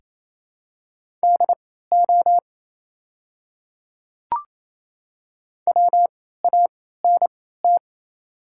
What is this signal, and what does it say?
Morse code, heard as a keyed tone, about 27 wpm, spaced as for 16 wpm, 700 Hz. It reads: DO WANT
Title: 3x Character Spacing - Top 200 Words - 28wpm